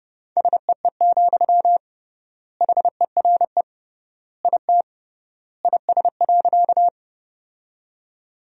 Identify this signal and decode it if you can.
Morse code, heard as a keyed tone, about 32 wpm, 700 Hz. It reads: SEE, HERE IT IS.
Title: 3x Word-Spacing - Sentences from Top 200 Words - 30wpm